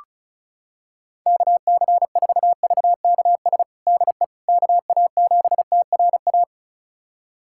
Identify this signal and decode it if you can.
Morse code, heard as a keyed tone, about 36 wpm, 700 Hz. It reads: KC4VKS DE KA7TRA
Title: USA Callsign DE Callsign 35wpm